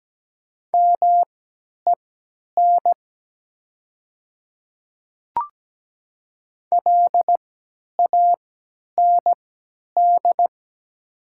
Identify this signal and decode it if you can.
Morse code, heard as a keyed tone, about 17 wpm, spaced as for 10 wpm, 690 Hz. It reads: MEN LAND
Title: 3x Character Spacing - Top 300 Words - 17wpm